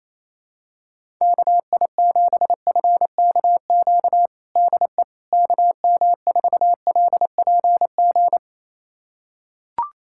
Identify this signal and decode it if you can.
Morse code, heard as a keyed tone, about 28 wpm, 700 Hz. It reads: KI7FKQ DE KM4LPG
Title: USA Callsign DE Callsign - 6 Characters - 28wpm